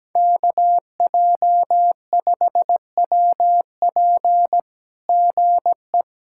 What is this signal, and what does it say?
Morse code, heard as a keyed tone, about 17 wpm, 700 Hz. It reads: KJ5WP GE
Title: SOTA QSO Practice 17wpm